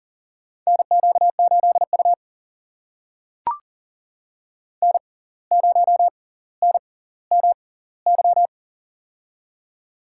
Morse code, heard as a keyed tone, about 40 wpm, spaced as for 19 wpm, 700 Hz. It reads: NQ8U N0NMY
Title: USA Call Signs - 6x Character Spacing - 40wpm